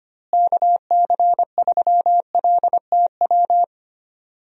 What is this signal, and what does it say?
Morse code, heard as a keyed tone, about 25 wpm, 700 Hz. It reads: KC3LTW
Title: USA Call Signs - 6 Characters - Repeated 3x - 25wpm